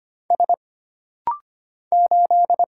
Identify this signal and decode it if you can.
Morse code, heard as a keyed tone, about 25 wpm, 700 Hz. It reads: S 8
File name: Single Character (Letter:Number:Punctuation:Prosign) - Rapid-Fire - 25wpm